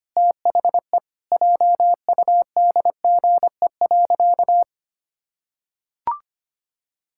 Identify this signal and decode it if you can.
Morse code, heard as a keyed tone, about 25 wpm, 700 Hz. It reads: THE JUDGE.
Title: Pangram Challenge v2 - 25wpm